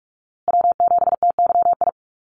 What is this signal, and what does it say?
Morse code, heard as a keyed tone, about 44 wpm, 700 Hz. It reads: W7TYI
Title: USA Call Signs Repeated 2x 45wpm